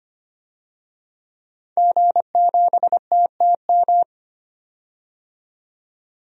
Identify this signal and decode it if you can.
Morse code, heard as a keyed tone, about 25 wpm, 700 Hz. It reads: G7TTM
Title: DXCC-100 Call Signs - 5 Characters - 25wpm